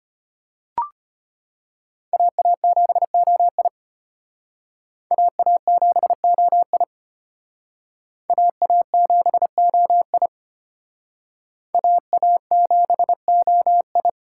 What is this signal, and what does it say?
Morse code, heard as a keyed tone, about 32 wpm, 700 Hz. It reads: AA7OI AA7OI AA7OI AA7OI